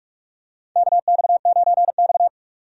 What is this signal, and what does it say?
Morse code, heard as a keyed tone, about 47 wpm, 700 Hz. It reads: KX9X
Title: USA Call Signs - 4x Character Spacing - 45wpm